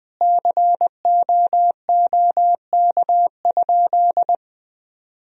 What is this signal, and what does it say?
Morse code, heard as a keyed tone, about 20 wpm, 700 Hz. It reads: COOK?
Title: Sentences from Top 1000 Words - Review Part 4 - 20wpm